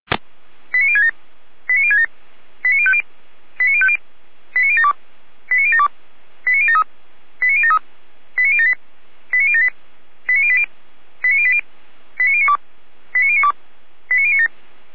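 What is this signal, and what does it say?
Five-tone selective call (ZVEI), heard as quick runs of five tones, tones of about 70 ms. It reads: 89067 89067 89050 89050 89072 89072 89073 89073 89078 89078 89080 89080 89092 89092 89097
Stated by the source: ZVEI five-tone call 89017 received by SDR.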